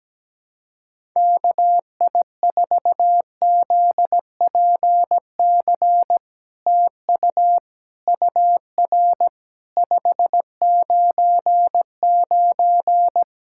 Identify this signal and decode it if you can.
Morse code, heard as a keyed tone, about 17 wpm, 700 Hz. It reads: KI4ZPC TU UR 599
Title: SOTA QSO Practice 17wpm